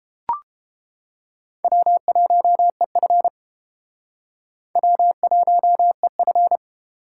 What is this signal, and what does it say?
Morse code, heard as a keyed tone, about 30 wpm, 700 Hz. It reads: W1EF W1EF